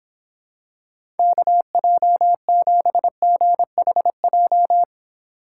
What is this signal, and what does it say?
Morse code, heard as a keyed tone, about 26 wpm, 700 Hz. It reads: KJ7GHJ